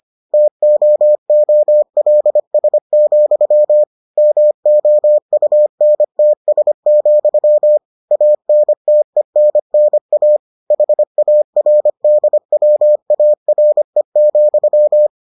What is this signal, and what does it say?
Morse code, heard as a keyed tone, about 25 wpm, 600 Hz. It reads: TOOLS, MOUNTS, ANTENNA HARDWARE,